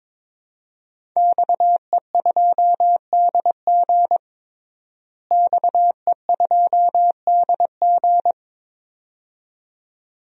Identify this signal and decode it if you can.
Morse code, heard as a keyed tone, about 23 wpm, 700 Hz. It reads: XE2DG XE2DG
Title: DXCC-100 Call Signs - 5 Characters - Repeated 2x - 22wpm